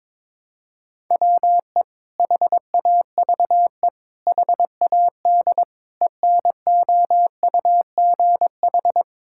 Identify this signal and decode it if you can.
Morse code, heard as a keyed tone, about 22 wpm, 700 Hz. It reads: WE HAVE HAD ENOUGH
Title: Sentences from Top 1000 Words - Review Part 5 - 22wpm